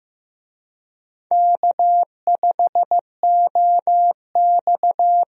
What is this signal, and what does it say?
Morse code, heard as a keyed tone, about 15 wpm, 700 Hz. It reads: K5OX
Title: USA Call Signs - 4 Characters - Repeated 3x - 15wpm